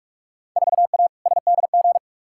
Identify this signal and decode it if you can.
Morse code, heard as a keyed tone, about 46 wpm, 700 Hz. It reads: 4A SDG
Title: Field Day Exchanges - 45wpm